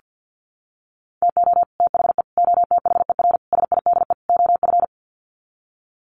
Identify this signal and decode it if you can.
Morse code, heard as a keyed tone, about 50 wpm, 700 Hz. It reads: TO THE OTHER SIDE OF
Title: Sets of 5 Words Encore 50wpm